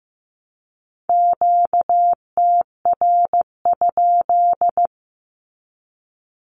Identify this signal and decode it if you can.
Morse code, heard as a keyed tone, about 15 wpm, 700 Hz. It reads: QTR?
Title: Q-Codes 15wpm